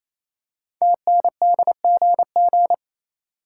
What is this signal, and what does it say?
Morse code, heard as a keyed tone, about 27 wpm, 700 Hz. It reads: TNDGG
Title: Code Groups - 5 Letters - 28wpm